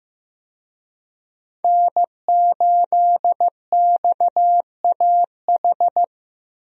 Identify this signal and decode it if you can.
Morse code, heard as a keyed tone, about 15 wpm, 700 Hz. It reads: N8XAH
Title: USA Call Signs - 5 Characters - Repeated 3x - 15wpm